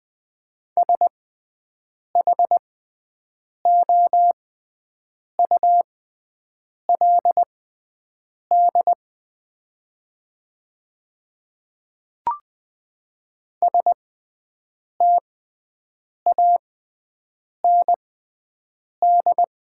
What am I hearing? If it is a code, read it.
Morse code, heard as a keyed tone, about 20 wpm, spaced as for 7 wpm, 690 Hz. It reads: SHOULD STAND
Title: 6x Character Spacing - Top 300 Words - 20wpm